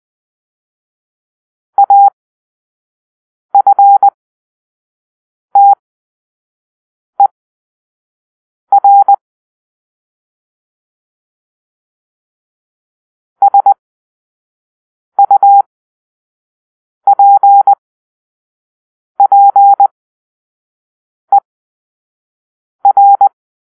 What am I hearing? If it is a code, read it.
Morse code, heard as a keyed tone, about 20 wpm, spaced as for 5 wpm, 800 Hz. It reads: AFTER SUPPER
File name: arrl / Grp-5 Drill-3